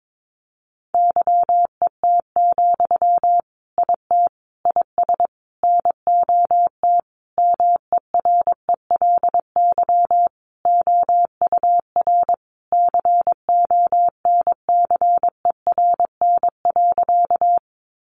Morse code, heard as a keyed tone, about 22 wpm, 700 Hz. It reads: YET, IT IS NOT MERELY OUR CONCERN.
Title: Sentences from Top 1400 Words 22wpm